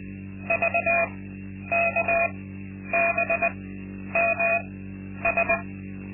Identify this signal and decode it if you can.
Morse code, heard as a keyed tone, about 20 wpm, spaced as for 10 wpm, 680 Hz. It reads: VKBMS
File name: Linksmakalnis-2